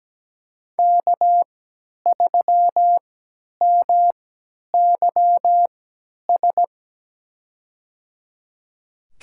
Morse code, heard as a keyed tone, about 17 wpm, spaced as for 10 wpm, 700 Hz. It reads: K3MYS
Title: USA Call Signs - 5 Characters - 3x Character Spacing - 17wpm